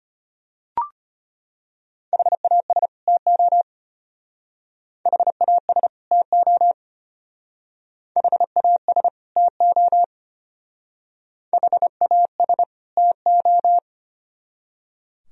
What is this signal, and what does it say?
Morse code, heard as a keyed tone, about 31 wpm, 700 Hz. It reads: HAS TO HAS TO HAS TO HAS TO
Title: Speed-Racing - Sets of 2 Words 25wpm